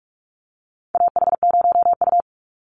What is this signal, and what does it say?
Morse code, heard as a keyed tone, about 44 wpm, 700 Hz. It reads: AH0U
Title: Call Signs - 45wpm